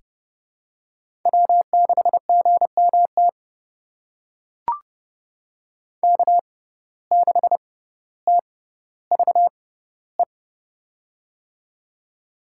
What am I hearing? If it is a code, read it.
Morse code, heard as a keyed tone, about 30 wpm, spaced as for 15 wpm, 700 Hz. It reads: W6GMT K6TVE
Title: USA Call Signs - 5 Characters - 6x Character Spacing - 30wpm